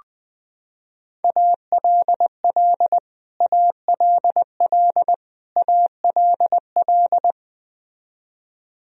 Morse code, heard as a keyed tone, about 21 wpm, 700 Hz. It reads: ALL ALL ALL
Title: Top 100 Words Repeated 20wpm